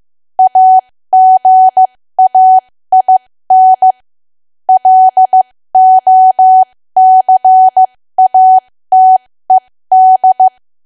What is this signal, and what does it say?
Morse code, heard as a keyed tone, about 15 wpm, spaced as for 13 wpm, 750 Hz. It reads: AGAIN LOCATED
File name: cwa / W1AW-Level2 / 13WPM